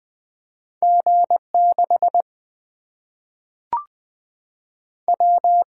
Morse code, heard as a keyed tone, about 20 wpm, 700 Hz. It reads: G 6 W
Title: Single-Letter Single-Number 20wpm